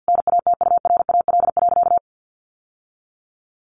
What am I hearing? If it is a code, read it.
Morse code, heard as a keyed tone, about 50 wpm, 700 Hz. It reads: NATURAL.
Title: Sentences from Top 700 Words 50wpm